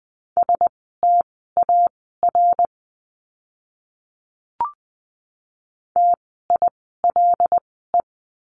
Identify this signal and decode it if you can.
Morse code, heard as a keyed tone, about 20 wpm, spaced as for 14 wpm, 700 Hz. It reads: STAR TILE
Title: 075 Lesson 13f - Letter L - Words 20wpm